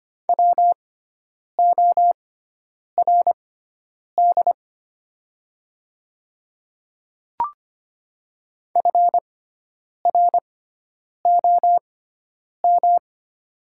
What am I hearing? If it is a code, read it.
Morse code, heard as a keyed tone, about 25 wpm, spaced as for 9 wpm, 700 Hz. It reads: WORD FROM
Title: 6x Character Spacing - Top 100 Words - 25wpm